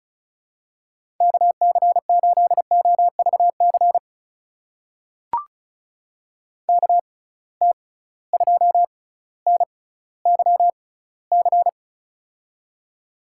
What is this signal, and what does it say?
Morse code, heard as a keyed tone, about 34 wpm, spaced as for 18 wpm, 700 Hz. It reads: KC8OVC KT2NYC
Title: USA Call Signs - 6 Characters - 6x Character Spacing - 35wpm